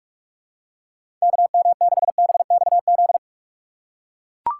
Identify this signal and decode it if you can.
Morse code, heard as a keyed tone, about 41 wpm, 700 Hz. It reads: KM6BXZ
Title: USA Call Signs - 6 Characters - 2x Character Spacing - 45wpm